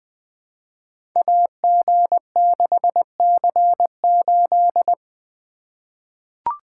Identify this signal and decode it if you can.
Morse code, heard as a keyed tone, about 20 wpm, 700 Hz. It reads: AG6C8
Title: Code Groups - 5 Letter:Numbers - 20wpm